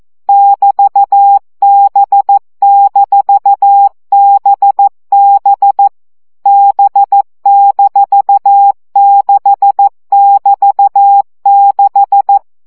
Morse code, heard as a keyed tone, about 14 wpm, 800 Hz. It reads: <BT>B-BB B-6<BT>6
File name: CW gyakorlas b 6 egyenlo minusz 12 baud